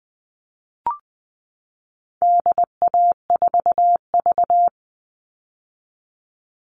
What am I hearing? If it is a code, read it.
Morse code, heard as a keyed tone, about 20 wpm, 700 Hz. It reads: DA4V
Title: All Cut Numbers - 4-Digit 20wpm